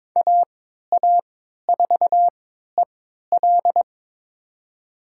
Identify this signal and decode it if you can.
Morse code, heard as a keyed tone, about 22 wpm, spaced as for 13 wpm, 700 Hz. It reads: AA4EL
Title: USA Call Signs - 5 Characters - 3x Character Spacing - 22wpm